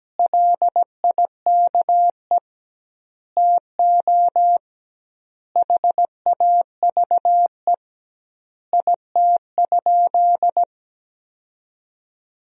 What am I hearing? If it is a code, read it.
Morse code, heard as a keyed tone, about 17 wpm, 700 Hz. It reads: LIKE TO HAVE IT?
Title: 2x Word-Spacing - Sentences from Top 100 Words - 17wpm